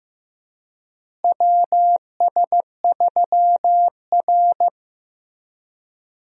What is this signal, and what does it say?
Morse code, heard as a keyed tone, about 15 wpm, 700 Hz. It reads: WS3R
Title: USA Call Signs - 4 Characters - Repeated 3x - 15wpm